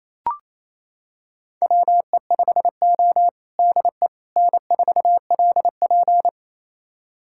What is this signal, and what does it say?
Morse code, heard as a keyed tone, about 28 wpm, 700 Hz. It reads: WE5O DE N4LP
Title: USA Callsign DE Callsign - 4 Characters - 28wpm